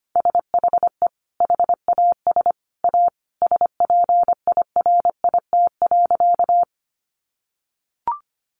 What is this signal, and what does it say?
Morse code, heard as a keyed tone, about 25 wpm, 700 Hz. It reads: SHE HAS A SPIRIT.
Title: Sentences from Top 1400 Words 25wpm